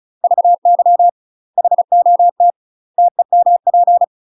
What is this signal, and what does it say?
Morse code, heard as a keyed tone, about 34 wpm, 700 Hz. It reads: VY HOT TEMP
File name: UK MASTER QSOs TRAINING / 35WPM / UK MASTER QSOs 0025